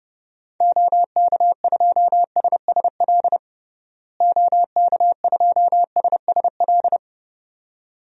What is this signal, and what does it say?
Morse code, heard as a keyed tone, about 30 wpm, 700 Hz. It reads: OK2SSL OK2SSL
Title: DXCC-100 Call Signs - 6 Characters - Repeated 2x - 30wpm